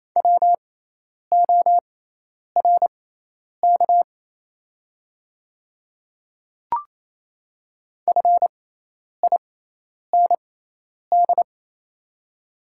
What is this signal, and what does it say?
Morse code, heard as a keyed tone, about 28 wpm, spaced as for 10 wpm, 700 Hz. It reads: WORK FIND
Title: 6x Character Spacing - Top 200 Words - 28wpm